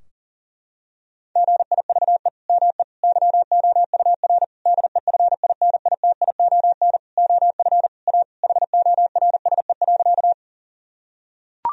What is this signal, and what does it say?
Morse code, heard as a keyed tone, about 40 wpm, 700 Hz. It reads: GIVE ME YOUR DEFINITION OF A HORSE.
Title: Sentences from Top 1000 Words - Review Part 4 - 40wpm